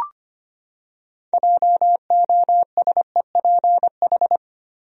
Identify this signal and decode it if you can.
Morse code, heard as a keyed tone, about 25 wpm, 700 Hz. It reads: JOSEPH
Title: USA Common Names - 25wpm